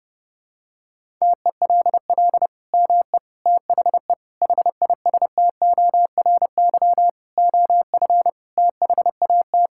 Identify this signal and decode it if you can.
Morse code, heard as a keyed tone, about 30 wpm, 700 Hz. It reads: TELL ME THE HISTORY OF THAT
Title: Sentences from Top 1000 Words - Review Part 5 - 30wpm